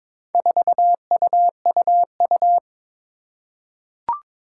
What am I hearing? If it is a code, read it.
Morse code, heard as a keyed tone, about 21 wpm, 700 Hz. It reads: 4UUU
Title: Four Challenging Letter-Number Combinations 22wpm